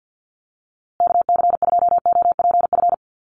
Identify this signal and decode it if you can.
Morse code, heard as a keyed tone, about 50 wpm, 700 Hz. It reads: KC2OPF
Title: USA Call Signs Repeated 2x 50wpm